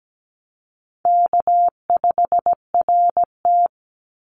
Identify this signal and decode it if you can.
Morse code, heard as a keyed tone, about 17 wpm, 700 Hz. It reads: K5RT
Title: Easy Call Signs - 17wpm